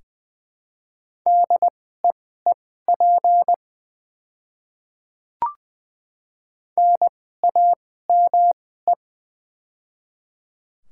Morse code, heard as a keyed tone, about 20 wpm, spaced as for 14 wpm, 690 Hz. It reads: DEEP NAME
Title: 2x Character Spacing - Top 500 Words - 20wpm